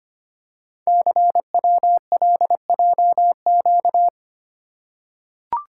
Morse code, heard as a keyed tone, about 25 wpm, 700 Hz. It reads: CWLJQ
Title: Code Groups - 5 Letters - 25wpm